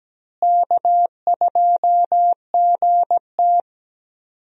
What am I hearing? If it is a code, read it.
Morse code, heard as a keyed tone, about 17 wpm, 700 Hz. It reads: K2GT
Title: USA Call Signs - 4 Characters - Repeated 3x - 17wpm